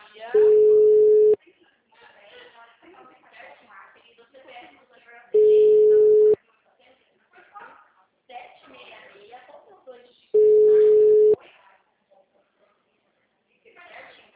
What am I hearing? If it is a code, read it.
Ringback tone (Europe, 425 Hz), with speech.
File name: out- -2010- - - .23694